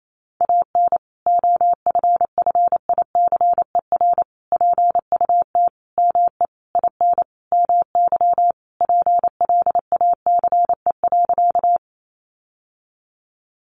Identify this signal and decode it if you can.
Morse code, heard as a keyed tone, about 28 wpm, 700 Hz. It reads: AN OFFICER PUT ME IN MY PLACE.
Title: Sentences from Top 1200 Words 28wpm